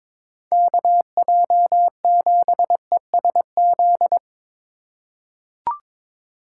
Morse code, heard as a keyed tone, about 21 wpm, 700 Hz. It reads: KJ7ESZ
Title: USA Call Signs - 6 Characters - 3x Character Spacing - 22wpm